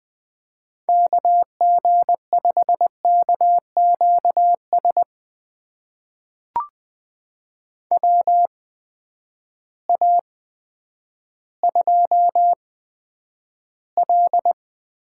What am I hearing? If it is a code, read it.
Morse code, heard as a keyed tone, about 21 wpm, spaced as for 10 wpm, 700 Hz. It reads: KG5KQS WA2L
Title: USA Call Signs - 6 Characters - 8x Character Spacing - 20wpm